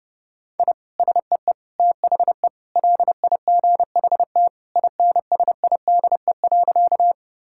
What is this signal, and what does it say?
Morse code, heard as a keyed tone, about 30 wpm, 700 Hz. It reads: I SEE THE LIGHT INSIDE.
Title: Sentences from Top 1000 Words - Review Part 6 - 30wpm